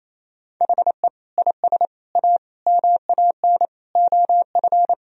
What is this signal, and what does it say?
Morse code, heard as a keyed tone, about 28 wpm, 700 Hz. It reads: HE IS A MAN OF
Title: Sentences from Top 1000 Words - Review Part 2 - 28wpm